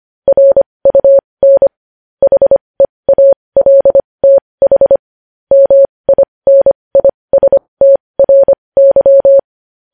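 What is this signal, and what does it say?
Morse code, heard as a keyed tone, about 25 wpm, spaced as for 20 wpm, 550 Hz. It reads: RUN HEALTH MINISTRY
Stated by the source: Morse code news headlines sent at 20 wpm.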